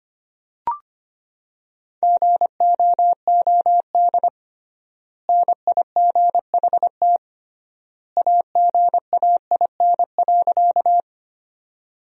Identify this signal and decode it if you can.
Morse code, heard as a keyed tone, about 25 wpm, 700 Hz. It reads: GOOD NIGHT AGAIN.
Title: 3x Word-Spacing - Sentences from Top 300 Words - 25wpm